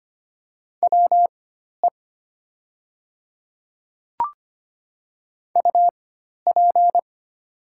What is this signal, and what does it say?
Morse code, heard as a keyed tone, about 25 wpm, spaced as for 12 wpm, 690 Hz. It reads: WE UP
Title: 4x Character Spacing - Top 300 Words - 25wpm